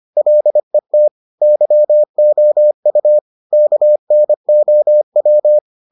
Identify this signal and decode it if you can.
Morse code, heard as a keyed tone, about 25 wpm, 600 Hz. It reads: LET YOU KNOW